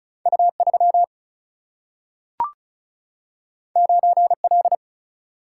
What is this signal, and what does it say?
Morse code, heard as a keyed tone, about 34 wpm, 700 Hz. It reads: U3 9L
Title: Single-Letter Single-Number 35wpm